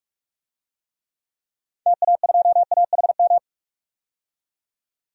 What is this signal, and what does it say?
Morse code, heard as a keyed tone, about 46 wpm, 700 Hz. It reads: TA2AHM
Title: DXCC-100 Call Signs - 6 Characters - 45wpm